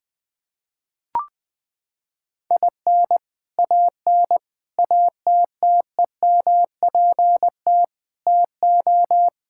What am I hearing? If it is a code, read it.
Morse code, heard as a keyed tone, about 20 wpm, 700 Hz. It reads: IN AN ATTEMPT TO
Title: Sentences with 4-Word Phrases - 20wpm